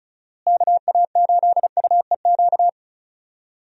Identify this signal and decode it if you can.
Morse code, heard as a keyed tone, about 36 wpm, 700 Hz. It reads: KA8UEQ
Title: USA Call Signs - 6 Characters - Repeated 3x - 35wpm